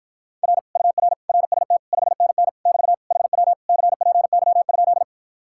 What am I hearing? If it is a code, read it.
Morse code, heard as a keyed tone, about 54 wpm, 700 Hz. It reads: R UR RST 5NN <BT> HW CPY?